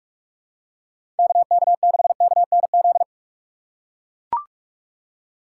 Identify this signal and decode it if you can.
Morse code, heard as a keyed tone, about 43 wpm, 700 Hz. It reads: KK6KNZ
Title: USA Call Signs - 6 Characters - 2x Character Spacing - 45wpm